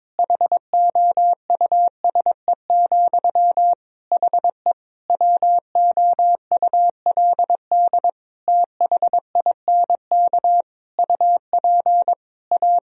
Morse code, heard as a keyed tone, about 21 wpm, 700 Hz. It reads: HOUSE, HE WOULD THINK UP A
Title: Sentences from Top 1000 Words - Review Part 2 - 22wpm